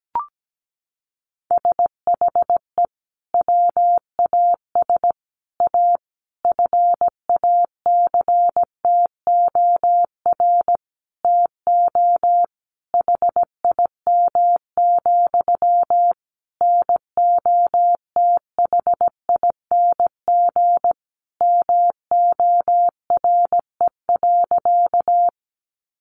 Morse code, heard as a keyed tone, about 17 wpm, 700 Hz. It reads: SHE WAS A FACTOR TO HIM, NOTHING MORE.
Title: Sentences from Top 1300 Words 17wpm